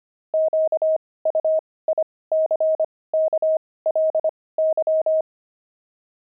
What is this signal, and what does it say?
Morse code, heard as a keyed tone, about 25 wpm, spaced as for 18 wpm, 630 Hz. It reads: QUICKLY
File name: pangram-fs-18-cs-25-wizards